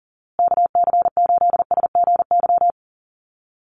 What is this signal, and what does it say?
Morse code, heard as a keyed tone, about 40 wpm, 700 Hz. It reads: KC8SGY
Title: USA Call Signs Repeated 2x 40wpm